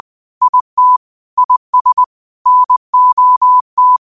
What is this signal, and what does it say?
Morse code, heard as a keyed tone, about 20 wpm, 1000 Hz. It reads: IT IS NOT
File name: qotdfiles / 20wpm